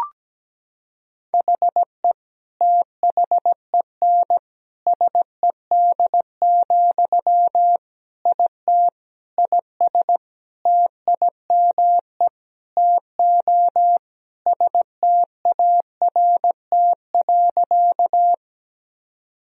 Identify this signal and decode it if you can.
Morse code, heard as a keyed tone, about 17 wpm, 700 Hz. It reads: HE THEN SED, IT IS TIME TO START.
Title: Sentences from Top 300 Words and Abbreviations 17wpm